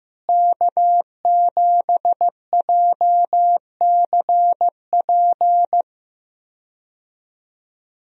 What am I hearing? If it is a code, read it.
Morse code, heard as a keyed tone, about 15 wpm, 700 Hz. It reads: K7JCP
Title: USA Call Signs - 5 Characters - Repeated 3x - 15wpm